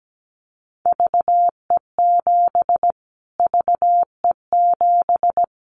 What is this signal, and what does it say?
Morse code, heard as a keyed tone, about 17 wpm, 700 Hz. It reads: VE7 VE7
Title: Callsign Prefixes for Top 25 DXCC - Repeated 3x - 17wpm